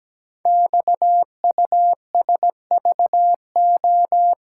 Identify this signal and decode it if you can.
Morse code, heard as a keyed tone, about 17 wpm, 700 Hz. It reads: XUSVO
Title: Code Groups - 5 Letter:Numbers - 17wpm